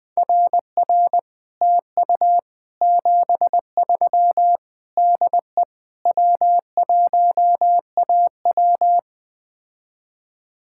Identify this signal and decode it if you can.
Morse code, heard as a keyed tone, about 20 wpm, 700 Hz. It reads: RR TU 73 DE W1AW
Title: Speed-Racing Common QSO Phrases - 17wpm